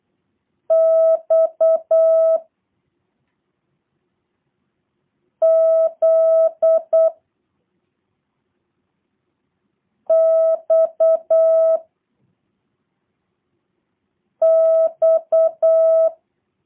Morse code, heard as a keyed tone, about 8 wpm, 650 Hz. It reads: X Z X X